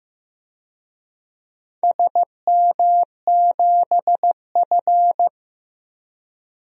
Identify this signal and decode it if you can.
Morse code, heard as a keyed tone, about 15 wpm, 700 Hz. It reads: SM7F
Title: DXCC-100 Call Signs - 4 Characters - 15wpm